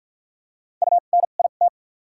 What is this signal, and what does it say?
Morse code, heard as a keyed tone, about 48 wpm, spaced as for 35 wpm, 700 Hz. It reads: UNIT
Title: 2x Character Spacing - Top 500 Words - 50wpm